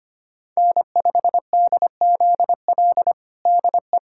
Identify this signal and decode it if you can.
Morse code, heard as a keyed tone, about 25 wpm, 700 Hz. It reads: N5DZL DE
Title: USA Callsign DE Callsign - 5 Characters - 25wpm